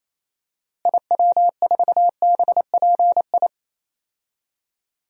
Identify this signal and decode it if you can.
Morse code, heard as a keyed tone, about 28 wpm, 700 Hz. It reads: IW4BPI
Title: DXCC-100 Call Signs - 6 Characters - Repeated 2x - 28wpm